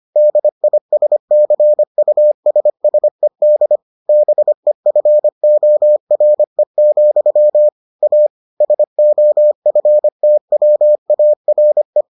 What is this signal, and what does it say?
Morse code, heard as a keyed tone, about 25 wpm, 600 Hz. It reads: DISCUSSED BEFORE, A SOFTWARE